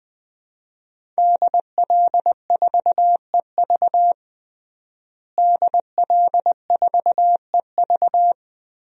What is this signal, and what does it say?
Morse code, heard as a keyed tone, about 21 wpm, 700 Hz. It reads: DL4EV DL4EV
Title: DXCC-100 Call Signs - 5 Characters - Repeated 3x - 20wpm